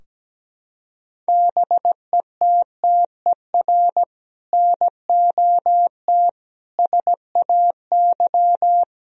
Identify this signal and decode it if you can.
Morse code, heard as a keyed tone, about 17 wpm, 700 Hz. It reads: BETTER NOT SAY
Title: Sentences from Top 1000 Words - Review Part 4 - 17wpm